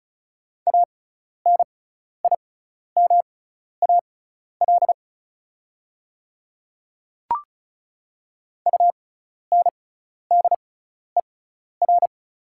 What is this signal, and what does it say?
Morse code, heard as a keyed tone, about 36 wpm, spaced as for 12 wpm, 690 Hz. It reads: ANIMAL UNDER
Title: 6x Character Spacing - Top 200 Words - 35wpm